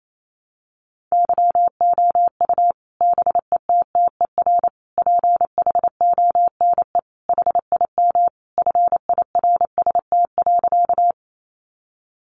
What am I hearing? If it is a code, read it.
Morse code, heard as a keyed tone, about 28 wpm, 700 Hz. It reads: YOU BETTER PHONE HIM FIRST.
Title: Sentences from Top 1100 Words 28wpm